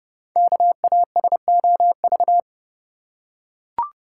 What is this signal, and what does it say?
Morse code, heard as a keyed tone, about 30 wpm, 700 Hz. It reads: KASOV
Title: Code Groups - 5 Letters - 30wpm